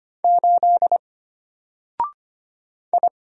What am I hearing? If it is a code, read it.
Morse code, heard as a keyed tone, about 25 wpm, 700 Hz. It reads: 8 I
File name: Single-Letter-Number-25wpm